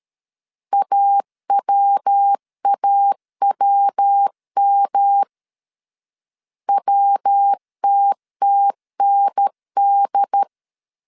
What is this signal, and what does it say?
Morse code, heard as a keyed tone, about 13 wpm, 800 Hz. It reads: AWAWM WTTND
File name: CW / DET / 1 3 1500C S C 12wpm 2750